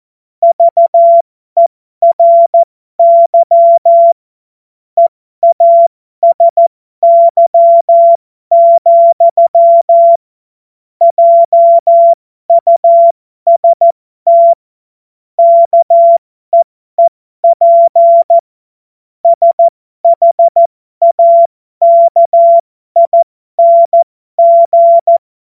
Morse code, heard as a keyed tone, about 14 wpm, spaced as for 12 wpm, 680 Hz. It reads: VERY EASY, JUST KEEP SHAKING
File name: Morsekurs / Witze